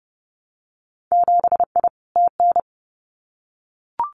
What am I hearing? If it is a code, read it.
Morse code, heard as a keyed tone, about 30 wpm, 700 Hz. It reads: 7I TN
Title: Winter Field Day Exchanges Encore 30wpm